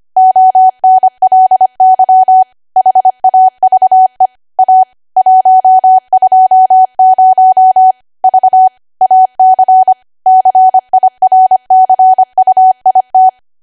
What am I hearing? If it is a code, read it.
Morse code, heard as a keyed tone, about 25 wpm, 750 Hz. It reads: ONLY HAVE A 120 V AC CIRCUIT
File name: Archive / 25 WPM / 25WPM